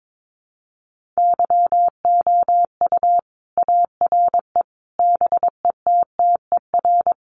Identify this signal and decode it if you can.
Morse code, heard as a keyed tone, about 22 wpm, 700 Hz. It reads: YOU ARE BETTER
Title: Sentences from Top 400 Words 22wpm